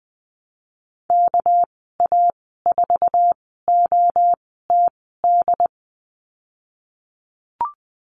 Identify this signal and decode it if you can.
Morse code, heard as a keyed tone, about 20 wpm, spaced as for 15 wpm, 700 Hz. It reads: KA4OTD
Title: 207 Lesson 33g - Letter j - Call Signs 20wpm